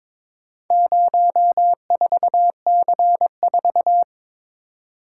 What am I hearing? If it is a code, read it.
Morse code, heard as a keyed tone, about 22 wpm, 700 Hz. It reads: 04C4
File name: practice / 22wpm / Four Letter-Number Combinations - 22wpm